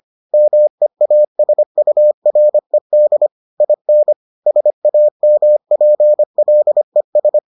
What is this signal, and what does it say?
Morse code, heard as a keyed tone, about 25 wpm, 600 Hz. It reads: MEASURED IN SAMPLES